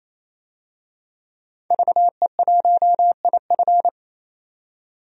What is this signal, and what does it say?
Morse code, heard as a keyed tone, about 28 wpm, 700 Hz. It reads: VE1IF
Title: DXCC-100 Call Signs - 5 Characters - 28wpm